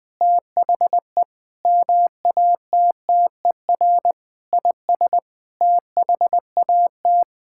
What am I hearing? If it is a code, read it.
Morse code, heard as a keyed tone, about 21 wpm, 700 Hz. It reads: THE MATTER IS THAT
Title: Sets of 8 Words - 20wpm